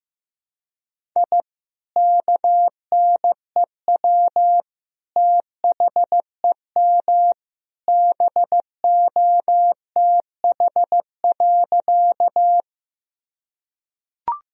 Morse code, heard as a keyed tone, about 15 wpm, 700 Hz. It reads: I KNEW THEM BOTH.
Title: Sentences from Top 1000 Words - Review Part 1 - 15wpm